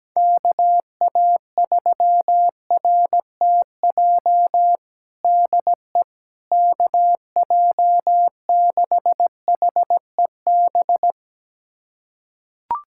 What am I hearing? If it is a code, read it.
Morse code, heard as a keyed tone, about 17 wpm, 700 Hz. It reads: KA3RTJ DE KJ6HEB
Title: USA Callsign DE Callsign 17wpm